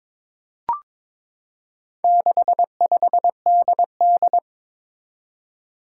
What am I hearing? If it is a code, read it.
Morse code, heard as a keyed tone, about 22 wpm, 700 Hz. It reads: 65DD
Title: Four Challenging Letter-Number Combinations 22wpm